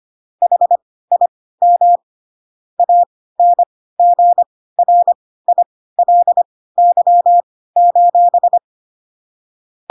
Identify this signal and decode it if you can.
Morse code, heard as a keyed tone, about 25 wpm, spaced as for 16 wpm, 700 Hz. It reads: HIM ANGRILY:
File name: Pinocchio / Pinocchio ch3 10 25 ik1hge